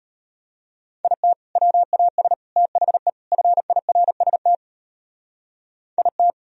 Morse code, heard as a keyed tone, about 33 wpm, 700 Hz. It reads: IT WAS THE FIRST IT